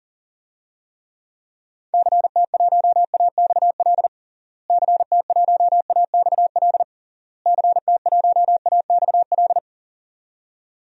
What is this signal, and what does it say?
Morse code, heard as a keyed tone, about 40 wpm, 700 Hz. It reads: CT1AXL CT1AXL CT1AXL
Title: DXCC-100 Call Signs - 6 Characters - Repeated 3x - 40wpm